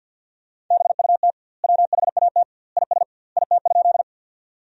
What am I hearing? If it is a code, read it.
Morse code, heard as a keyed tone, about 50 wpm, 700 Hz. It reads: BUT WHAT IS IT?
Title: 2x Word-Spacing - Sentences from Top 100 Words - 50wpm